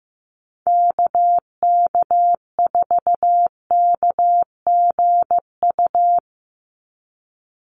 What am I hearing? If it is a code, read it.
Morse code, heard as a keyed tone, about 15 wpm, 700 Hz. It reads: KK4KGU
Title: USA Call Signs 15wpm